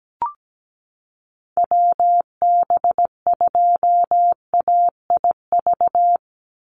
Morse code, heard as a keyed tone, about 17 wpm, 700 Hz. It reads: WB2AIV
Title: Call Signs Repeated 2x - 17wpm